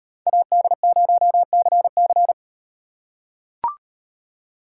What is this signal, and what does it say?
Morse code, heard as a keyed tone, about 38 wpm, 700 Hz. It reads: AD0CC